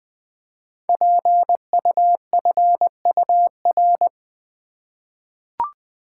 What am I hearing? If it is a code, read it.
Morse code, heard as a keyed tone, about 20 wpm, 700 Hz. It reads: PUFUR
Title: Code Groups - 5 Letters - 20wpm